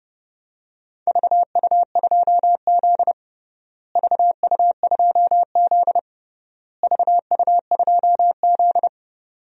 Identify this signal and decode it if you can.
Morse code, heard as a keyed tone, about 31 wpm, 700 Hz. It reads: VU2Z VU2Z VU2Z
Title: DXCC-100 Call Signs - 4 Characters - Repeated 3x - 30wpm